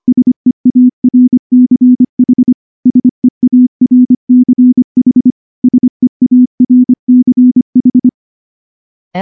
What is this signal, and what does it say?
Morse code, heard as a keyed tone, about 24 wpm, 250 Hz. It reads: SEARCH SEARCH SEARCH
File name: morse / morse-training-25WPM